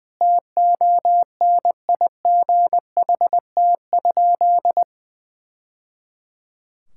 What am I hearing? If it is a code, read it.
Morse code, heard as a keyed tone, about 21 wpm, 700 Hz. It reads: TONIGHT?
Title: Sentences from Top 1000 Words - Review Part 5 - 20wpm